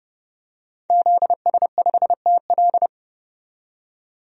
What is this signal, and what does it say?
Morse code, heard as a keyed tone, about 32 wpm, 700 Hz. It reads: ZS5TL
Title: DXCC-100 Call Signs - 5 Characters - 30wpm